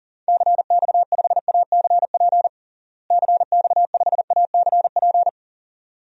Morse code, heard as a keyed tone, about 40 wpm, 700 Hz. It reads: CX5ACP CX5ACP
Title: DXCC-100 Call Signs - 6 Characters - Repeated 2x - 40wpm